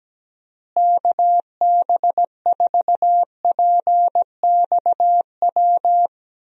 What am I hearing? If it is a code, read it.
Morse code, heard as a keyed tone, about 17 wpm, 700 Hz. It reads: KB4PXW
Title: USA Call Signs - 6 Characters - Repeated 2x - 17wpm